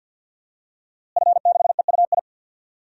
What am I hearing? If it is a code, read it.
Morse code, heard as a keyed tone, about 52 wpm, 700 Hz. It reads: F6EUI
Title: DXCC-100 Call Signs - 5 Characters - Repeated 2x - 50wpm